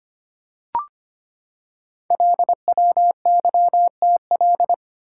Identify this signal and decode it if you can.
Morse code, heard as a keyed tone, about 25 wpm, 700 Hz. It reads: LWYTL
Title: Code Groups - 5 Letters - 25wpm